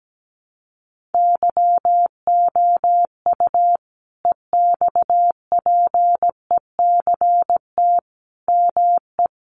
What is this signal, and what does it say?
Morse code, heard as a keyed tone, about 17 wpm, 700 Hz. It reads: YOU EXPECT ME
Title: Sentences from Top 1100 Words 17wpm